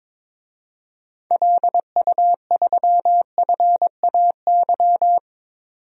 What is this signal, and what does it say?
Morse code, heard as a keyed tone, about 22 wpm, 700 Hz. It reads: LU3FAY
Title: DXCC-100 Call Signs - 6 Characters - 22wpm